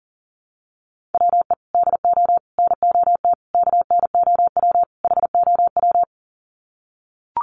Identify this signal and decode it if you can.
Morse code, heard as a keyed tone, about 40 wpm, 700 Hz. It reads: WE DO NOT KNOW HOW
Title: Sets of 5 Words Encore 40wpm